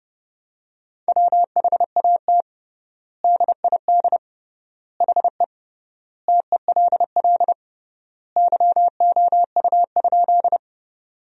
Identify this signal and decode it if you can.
Morse code, heard as a keyed tone, about 30 wpm, 700 Hz. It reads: WHAT DID HE TELL YOU?
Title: 3x Word-Spacing - Sentences from Top 200 Words - 30wpm